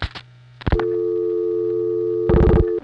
Dial tone (North America), 350 + 440 Hz.